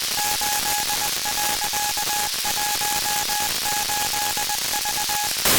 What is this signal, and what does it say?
Morse code, heard as a keyed tone, about 20 wpm, 830 Hz, with noise. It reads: 8.18V
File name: DSAT-FM-CW-XQUAD-AIRSPY-01